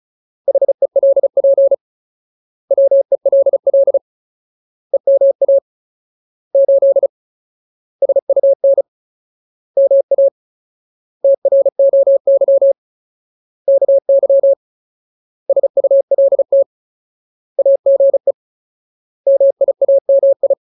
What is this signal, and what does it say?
Morse code, heard as a keyed tone, about 34 wpm, 550 Hz. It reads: HELP WELL EMA 8 SUN MA TROY KY SULT AGE MIAMI
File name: Ultimate Word List / 35WPM Ultimate